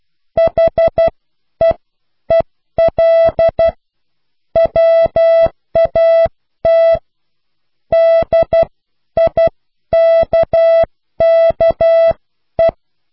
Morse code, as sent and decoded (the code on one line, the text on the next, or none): .... . .-.. / .-- .- - / -.. .. -.- -.- .
HEEL WAT DIKKE